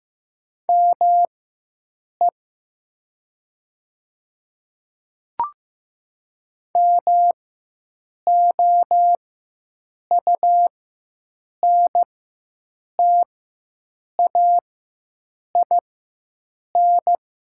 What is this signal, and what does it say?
Morse code, heard as a keyed tone, about 15 wpm, spaced as for 7 wpm, 700 Hz. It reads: ME MOUNTAIN
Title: 4x Character Spacing - Top 500 Words - 15wpm